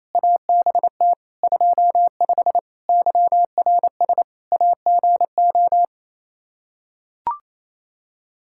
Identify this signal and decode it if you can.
Morse code, heard as a keyed tone, about 28 wpm, 700 Hz. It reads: ABT 25 YRS AGO
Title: Practice QSO - Exchange 3 - More Info - 28wpm